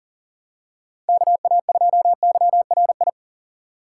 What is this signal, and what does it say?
Morse code, heard as a keyed tone, about 37 wpm, 700 Hz. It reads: KA2YRI
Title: USA Call Signs - 2x Character Spacing - 40wpm